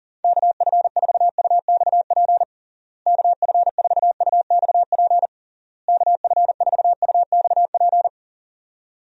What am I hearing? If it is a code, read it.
Morse code, heard as a keyed tone, about 40 wpm, 700 Hz. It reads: KF4UXP KF4UXP KF4UXP
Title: USA Call Signs - 6 Characters - Repeated 3x - 40wpm